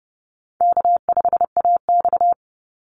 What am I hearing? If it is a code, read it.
Morse code, heard as a keyed tone, about 30 wpm, 700 Hz. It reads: K5AX
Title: Easy Call Signs - 30wpm